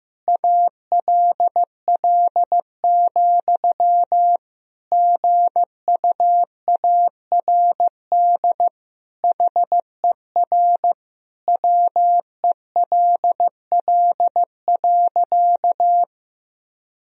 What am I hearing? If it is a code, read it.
Morse code, heard as a keyed tone, about 15 wpm, 700 Hz. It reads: ALL, GUARD HER WELL.
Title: Sentences from Top 1000 Words - Review Part 4 - 15wpm